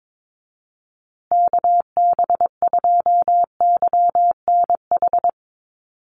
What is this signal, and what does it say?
Morse code, heard as a keyed tone, about 22 wpm, 700 Hz. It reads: KB2YNH
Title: USA Call Signs Repeated 3x 22wpm